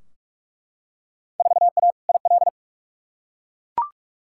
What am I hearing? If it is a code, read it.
Morse code, heard as a keyed tone, about 44 wpm, 700 Hz. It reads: 4A IL
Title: Field Day Exchanges - 45wpm